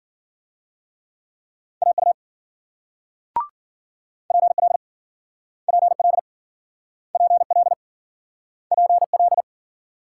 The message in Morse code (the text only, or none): AU PL PL PL PL